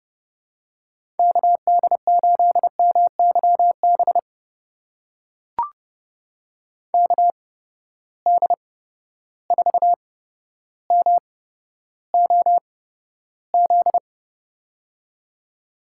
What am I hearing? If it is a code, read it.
Morse code, heard as a keyed tone, about 30 wpm, spaced as for 13 wpm, 700 Hz. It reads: KD8MYB KD4MOZ
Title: USA Call Signs - 6 Characters - 8x Character Spacing - 30wpm